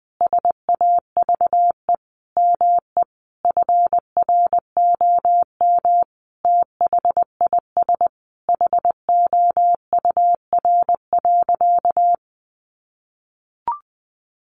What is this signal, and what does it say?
Morse code, heard as a keyed tone, about 20 wpm, 700 Hz. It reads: SAVE ME FROM THIS HOUR.
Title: Sentences from Top 1100 Words 20wpm